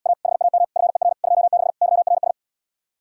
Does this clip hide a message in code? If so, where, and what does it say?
Yes: Morse code from 0.1 to 2.3 s, heard as a keyed tone, about 77 wpm, 700 Hz. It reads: A HAM FER 26 YRS